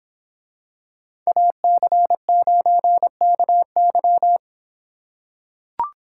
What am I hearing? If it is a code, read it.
Morse code, heard as a keyed tone, about 26 wpm, 700 Hz. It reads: AC9KY